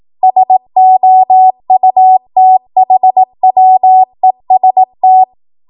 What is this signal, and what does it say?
Morse code, heard as a keyed tone, about 18 wpm, 750 Hz. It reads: SOUTHWEST
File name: Morse / 18WPM